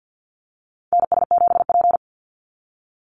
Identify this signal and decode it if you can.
Morse code, heard as a keyed tone, about 50 wpm, 700 Hz. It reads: NS7P
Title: NA SOTA Activator Callsigns 50wpm